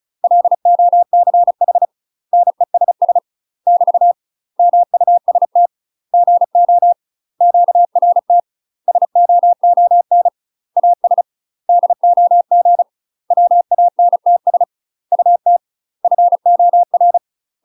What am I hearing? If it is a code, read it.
Morse code, heard as a keyed tone, about 35 wpm, 700 Hz. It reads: LOCH NESS <BT> MUST GO QRT SOON AS DOG WANTS UT FOR